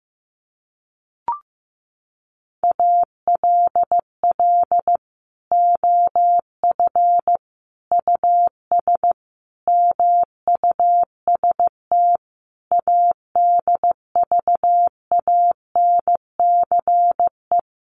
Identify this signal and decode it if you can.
Morse code, heard as a keyed tone, about 15 wpm, 700 Hz. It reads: ALL OF US MUST ADVANCE
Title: Sentences from Top 1400 Words 15wpm